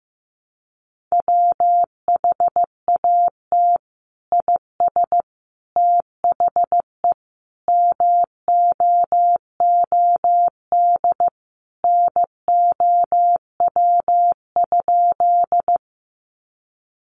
Morse code, heard as a keyed tone, about 15 wpm, 700 Hz. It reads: WHAT IS THE MOOD NOW?
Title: Sentences from Top 1100 Words 15wpm